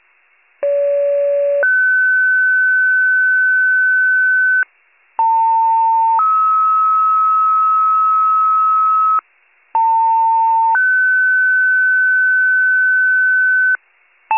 Two-tone page, heard as a one-second tone, then a three-second tone, about 570 then 1500 Hz, about 900 then 1290 Hz, about 900 then 1500 Hz. It is heard from 0.6 to 13.8 s.